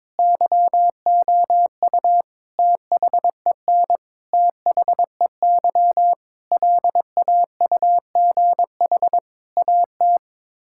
Morse code, heard as a keyed tone, about 21 wpm, 700 Hz. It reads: YOU THEN THEY LAUGH AT
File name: practice / 22wpm / Sets of 8 Words - 22wpm